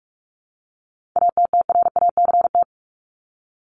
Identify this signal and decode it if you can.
Morse code, heard as a keyed tone, about 44 wpm, 700 Hz. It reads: ATTRACT
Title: Top 1501-1750 Words 45wpm